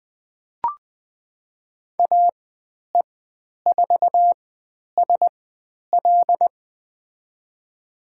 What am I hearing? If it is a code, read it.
Morse code, heard as a keyed tone, about 20 wpm, spaced as for 10 wpm, 700 Hz. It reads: AE4SL
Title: 109 Lesson 18g - Letter m - Call Signs 20wpm